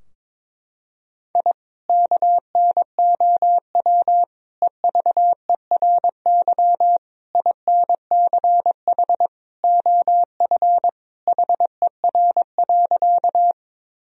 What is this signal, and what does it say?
Morse code, heard as a keyed tone, about 22 wpm, 700 Hz. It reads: I KNOW EVERY INCH OF HER.
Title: Sentences from Top 1000 Words - Review Part 3 - 22wpm